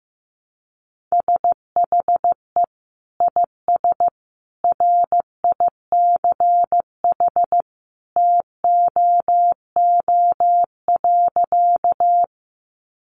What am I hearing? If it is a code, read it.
Morse code, heard as a keyed tone, about 15 wpm, 700 Hz. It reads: SHE IS RICH TOO.
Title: Sentences from Top 1100 Words 15wpm